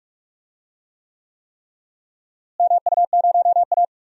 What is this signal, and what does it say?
Morse code, heard as a keyed tone, about 46 wpm, 700 Hz. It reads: MU0A
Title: DXCC-100 Call Signs - 4 Characters - Repeated 3x - 45wpm